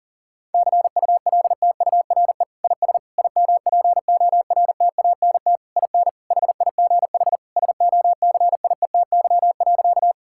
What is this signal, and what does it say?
Morse code, heard as a keyed tone, about 40 wpm, 700 Hz. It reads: CULTURE IS IMPORTANT IN HIGH SOCIETY.
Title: Sentences from Top 1000 Words - Review Part 6 - 40wpm